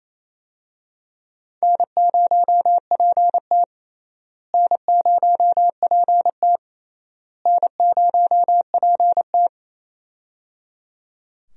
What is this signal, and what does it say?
Morse code, heard as a keyed tone, about 28 wpm, 700 Hz. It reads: N0PT N0PT N0PT
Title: USA Call Signs - 4 Characters - Repeated 3x - 28wpm